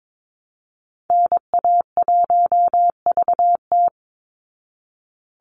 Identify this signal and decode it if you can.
Morse code, heard as a keyed tone, about 22 wpm, 690 Hz. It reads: NA1VT
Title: Call Signs Repeated 3x - 22wpm